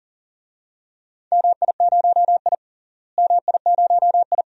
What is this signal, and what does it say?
Morse code, heard as a keyed tone, about 40 wpm, 700 Hz. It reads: MI0I MI0I
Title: DXCC-100 Call Signs - 4 Characters - Repeated 2x - 40wpm